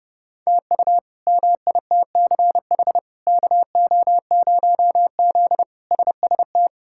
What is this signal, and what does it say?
Morse code, heard as a keyed tone, about 32 wpm, 700 Hz. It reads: TU MITCH KO0Z SST
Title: K1USN SST QSO Practice - 30wpm